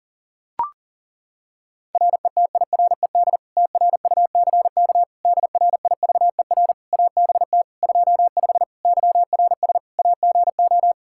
Morse code, heard as a keyed tone, about 37 wpm, 700 Hz. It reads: RETIRED TRUCK DRIVER ABT 25 YRS AGO
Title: Practice QSO - Exchange 3 - More Info - 40wpm